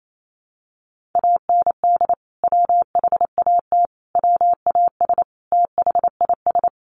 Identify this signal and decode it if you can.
Morse code, heard as a keyed tone, about 28 wpm, 700 Hz. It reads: AND WHAT WAS THIS
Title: Sentences from Top 1500 Words 28wpm